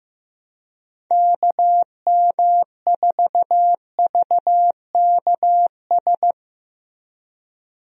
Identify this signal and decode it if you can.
Morse code, heard as a keyed tone, about 15 wpm, 700 Hz. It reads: KM4VKS
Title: USA Call Signs - 6 Characters - Repeated 2x - 15wpm